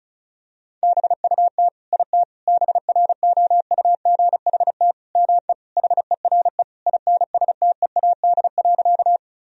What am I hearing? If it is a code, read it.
Morse code, heard as a keyed tone, about 36 wpm, 700 Hz. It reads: BUT IT BROUGHT ME HERE INSTEAD.
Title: Sentences from Top 1000 Words - Review Part 4 - 35wpm